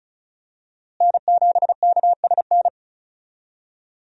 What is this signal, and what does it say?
Morse code, heard as a keyed tone, about 36 wpm, 700 Hz. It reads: N7KSN
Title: USA Call Signs - 5 Characters - Repeated 2x - 35wpm